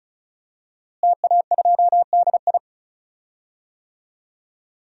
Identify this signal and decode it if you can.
Morse code, heard as a keyed tone, about 36 wpm, 700 Hz. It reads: TA2DI
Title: DXCC-100 Call Signs - 5 Characters - Repeated 2x - 35wpm